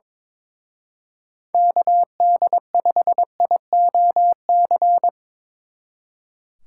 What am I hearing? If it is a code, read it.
Morse code, heard as a keyed tone, about 25 wpm, 700 Hz. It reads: KD5IOC
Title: Speed Racing - USA 2x3 Call Signs 22wpm